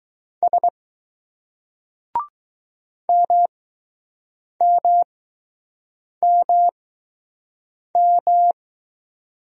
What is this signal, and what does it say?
Morse code, heard as a keyed tone, about 20 wpm, spaced as for 6 wpm, 700 Hz. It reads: S MMMM